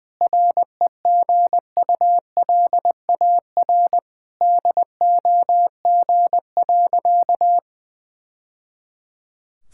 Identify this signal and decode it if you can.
Morse code, heard as a keyed tone, about 20 wpm, 700 Hz. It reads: REGULAR DOG.
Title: Sentences from Top 1000 Words - Review Part 6 - 20wpm